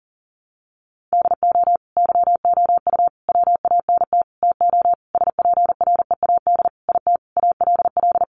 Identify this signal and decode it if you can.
Morse code, heard as a keyed tone, about 40 wpm, 700 Hz. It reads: DO YOU WANT TO SPREAD IT ALL
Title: Sentences from Top 1200 Words 40wpm